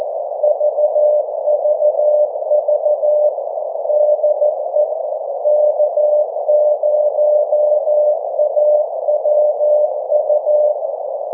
Morse code, heard as a keyed tone, about 14 wpm, 600 Hz. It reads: VVV DE K0AWU
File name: 1899khz / k0awu 1742ut-24jan21